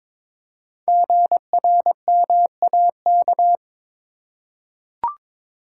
Morse code, heard as a keyed tone, about 23 wpm, 700 Hz. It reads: GRMAK
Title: Code Groups - 5 Letters - 22wpm